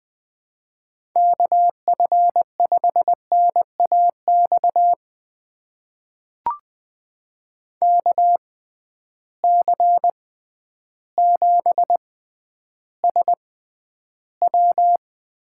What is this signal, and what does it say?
Morse code, heard as a keyed tone, about 20 wpm, spaced as for 11 wpm, 700 Hz. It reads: KF5NAX KC7SW
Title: USA Call Signs - 6 Characters - 6x Character Spacing - 20wpm